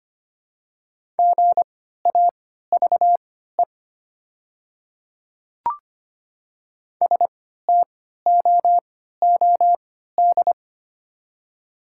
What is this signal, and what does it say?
Morse code, heard as a keyed tone, about 25 wpm, spaced as for 14 wpm, 700 Hz. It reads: GAVE STOOD
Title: 3x Character Spacing - Top 500 Words - 25wpm